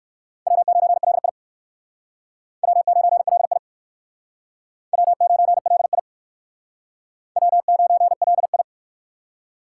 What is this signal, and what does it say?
Morse code, heard as a keyed tone, about 58 wpm, 700 Hz. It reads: W9LI W9LI W9LI W9LI